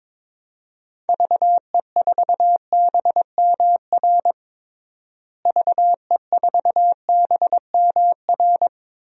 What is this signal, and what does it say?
Morse code, heard as a keyed tone, about 22 wpm, 700 Hz. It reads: VE4BMR VE4BMR
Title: DXCC-100 Call Signs - 6 Characters - Repeated 3x - 22wpm